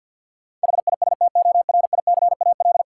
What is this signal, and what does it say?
Morse code, heard as a keyed tone, about 46 wpm, 700 Hz. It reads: HISTORICAL
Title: Sentences from Top 600 Words - New Word Drill - 50wpm